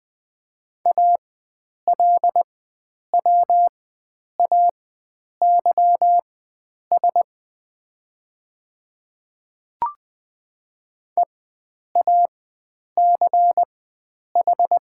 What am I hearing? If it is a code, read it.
Morse code, heard as a keyed tone, about 21 wpm, spaced as for 9 wpm, 700 Hz. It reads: ALWAYS EACH
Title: 4x Character Spacing - Top 300 Words - 20wpm